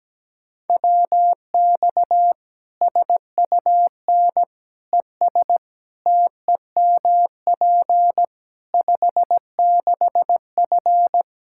Morse code, heard as a keyed tone, about 17 wpm, 700 Hz. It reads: WX SUN ES TEMP 56F
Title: Practice QSO - Exchange 2 - Info - 17wpm